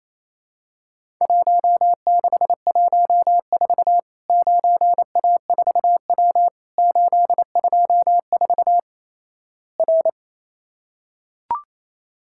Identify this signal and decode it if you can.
Morse code, heard as a keyed tone, about 28 wpm, 700 Hz. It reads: 1614 9A4W 824 R
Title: Worked All Europe DX Contest - QTC Exchange - 28wpm